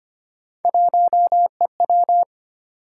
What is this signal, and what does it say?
Morse code, heard as a keyed tone, about 25 wpm, 700 Hz. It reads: 1EW
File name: Three Letter-Number Combinations - 25wpm